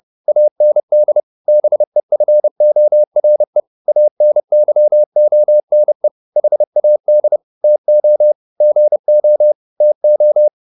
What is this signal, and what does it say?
Morse code, heard as a keyed tone, about 30 wpm, 600 Hz. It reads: AND BEFORE ANYONE HAD TO GO TO
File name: the old man and the sea 2 30wpm